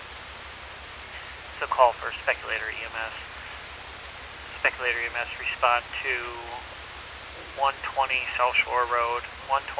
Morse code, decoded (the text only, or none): none